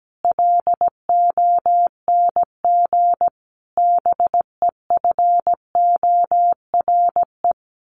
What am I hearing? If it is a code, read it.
Morse code, heard as a keyed tone, about 17 wpm, 700 Hz. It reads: LONG BEFORE
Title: Sets of 5 Words Encore2 17wpm